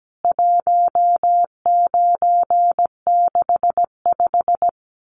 Morse code, heard as a keyed tone, about 17 wpm, 700 Hz. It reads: 1965
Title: CWT Exchange 17wpm